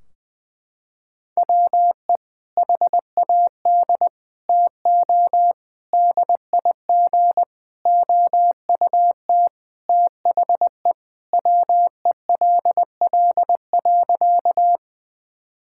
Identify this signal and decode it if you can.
Morse code, heard as a keyed tone, about 20 wpm, 700 Hz. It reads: WE HAD TO DIG OUT THE WELL.
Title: Sentences from Top 1600 Words 20wpm